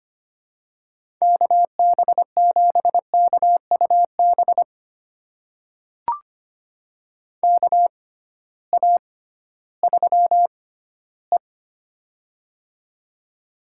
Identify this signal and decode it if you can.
Morse code, heard as a keyed tone, about 25 wpm, spaced as for 14 wpm, 700 Hz. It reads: KB7KUB KA3E
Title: USA Call Signs - 6x Character Spacing - 25wpm